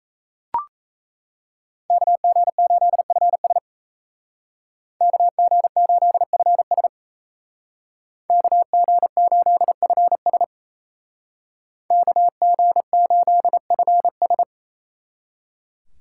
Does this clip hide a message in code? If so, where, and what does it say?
Yes: Morse code from 1.9 to 14.4 s, heard as a keyed tone, about 35 wpm, 700 Hz. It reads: KG8FS KG8FS KG8FS KG8FS